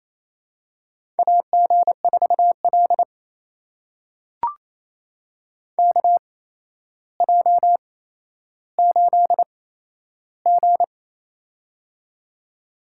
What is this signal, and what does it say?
Morse code, heard as a keyed tone, about 29 wpm, spaced as for 12 wpm, 700 Hz. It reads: AG4L KJ8G
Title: USA Call Signs - 4 Characters - 8x Character Spacing - 28wpm